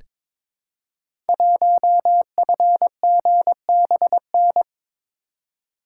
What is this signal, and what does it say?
Morse code, heard as a keyed tone, about 22 wpm, 700 Hz. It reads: 1FGBN
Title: Code Groups - 5 Letter:Numbers - 22wpm